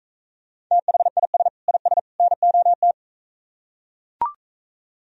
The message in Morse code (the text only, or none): THIS IS NOT